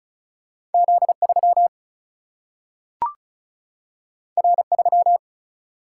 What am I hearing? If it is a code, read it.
Morse code, heard as a keyed tone, about 36 wpm, 700 Hz. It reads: Z 3 R 3
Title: Single-Letter Single-Number 35wpm